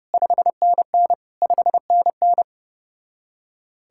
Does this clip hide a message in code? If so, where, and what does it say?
Yes: Morse code from 0.1 to 2.4 s, heard as a keyed tone, about 33 wpm, 700 Hz. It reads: 5NN 5NN